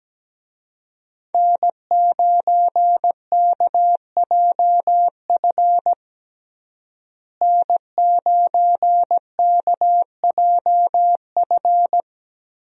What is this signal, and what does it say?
Morse code, heard as a keyed tone, about 17 wpm, 700 Hz. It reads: N9KJF N9KJF
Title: USA Call Signs - 5 Characters - Repeated 3x - 17wpm